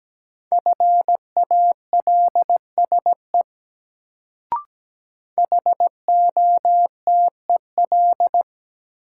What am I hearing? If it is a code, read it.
Morse code, heard as a keyed tone, about 17 wpm, 700 Hz. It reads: FALSE HOTEL
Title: Sentences from Top 1600 Words - New Word Drill - 17wpm